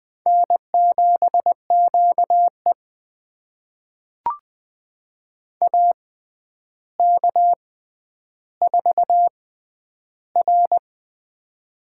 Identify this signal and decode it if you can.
Morse code, heard as a keyed tone, about 20 wpm, spaced as for 10 wpm, 700 Hz. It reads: N7QE AK4R
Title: USA Call Signs - 4 Characters - 6x Character Spacing - 20wpm